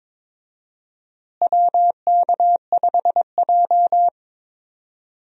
Morse code, heard as a keyed tone, about 21 wpm, 700 Hz. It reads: WK5J
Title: USA Call Signs - 4 Characters - Repeated 3x - 22wpm